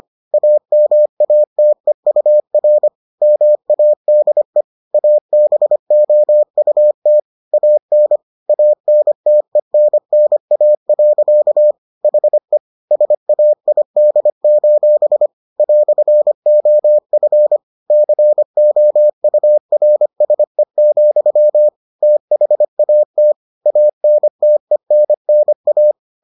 Morse code, heard as a keyed tone, about 25 wpm, 600 Hz. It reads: AMATEUR MADE ABOUT AN ANTENNA. HE SAID: "OF COURSE, THAT ANTENNA